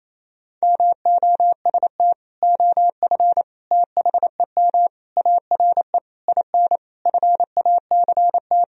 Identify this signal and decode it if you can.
Morse code, heard as a keyed tone, about 28 wpm, 700 Hz. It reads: MOST OF THEM ARE IN FACT
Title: Sentences from Top 1000 Words - Review Part 4 - 28wpm